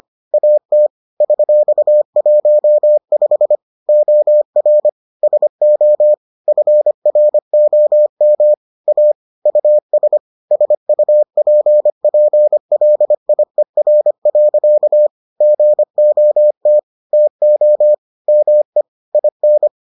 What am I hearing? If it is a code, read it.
Morse code, heard as a keyed tone, about 25 wpm, 600 Hz. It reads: AT $15 OR SO FROM A US SUPPLIER. GOT TO ME IN